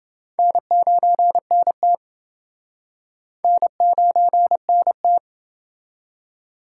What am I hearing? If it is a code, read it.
Morse code, heard as a keyed tone, about 29 wpm, 700 Hz. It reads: N9NT N9NT